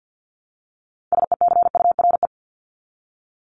Morse code, heard as a keyed tone, about 50 wpm, 700 Hz. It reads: SECURE
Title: Top 1001-1250 Words 50wpm